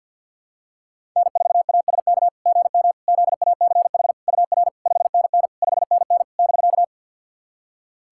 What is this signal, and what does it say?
Morse code, heard as a keyed tone, about 46 wpm, 690 Hz. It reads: N4ASK GM ZACH UR 5NN 5NN <BK>
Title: SOTA QSO Practice 50wpm